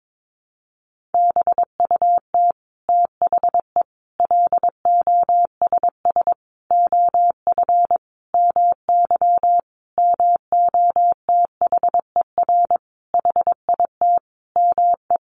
Morse code, heard as a keyed tone, about 22 wpm, 700 Hz. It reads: BUT THE LOSS OF MY MOTHER HIT ME
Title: Sentences from Top 1400 Words 22wpm